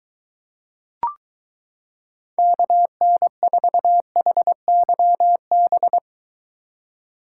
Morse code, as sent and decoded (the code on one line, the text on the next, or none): -.- -. ....- .... -.-- -...
KN4HYB